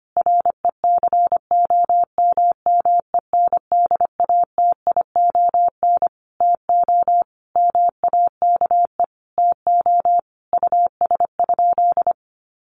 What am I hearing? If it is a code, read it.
Morse code, heard as a keyed tone, about 25 wpm, 700 Hz. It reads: RECOMMENDATION TO MAKE TO US?
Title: Sentences from Top 1400 Words 25wpm